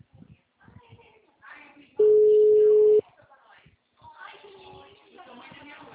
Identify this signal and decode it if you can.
Ringback tone (Europe, 425 Hz), with speech.